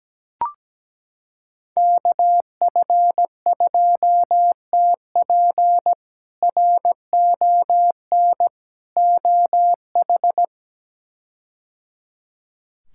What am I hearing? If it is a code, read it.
Morse code, heard as a keyed tone, about 17 wpm, 700 Hz. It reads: KF2TP RON OH
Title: K1USN SST QSO Practice - 17wpm